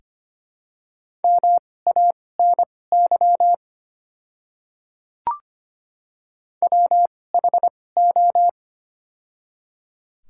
Morse code, heard as a keyed tone, about 25 wpm, spaced as for 18 wpm, 690 Hz. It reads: MANY WHO
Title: 2x Character Spacing - Top 300 Words - 25wpm